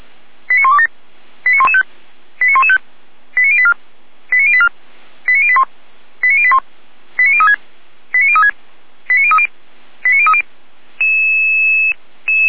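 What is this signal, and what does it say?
Five-tone selective call (ZVEI), heard as quick runs of five tones, tones of about 80 ms. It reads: 89127 89115 89115 89084 89084 89081 89081 89036 89036 89030 89030
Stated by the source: Five-tone ZVEI selective call received by SDR, from feuerwehr-orlamuende.org fire brigade audio server.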